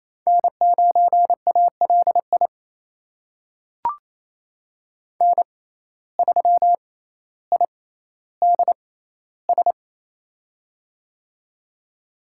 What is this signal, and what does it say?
Morse code, heard as a keyed tone, about 28 wpm, spaced as for 14 wpm, 700 Hz. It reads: N9ALI N3IDS
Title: USA Call Signs - 5 Characters - 6x Character Spacing - 28wpm